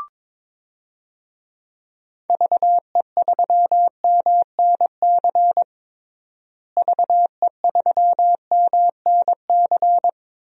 Morse code, heard as a keyed tone, about 22 wpm, 700 Hz. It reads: VE3MNC VE3MNC
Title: DXCC-100 Call Signs - 6 Characters - Repeated 3x - 22wpm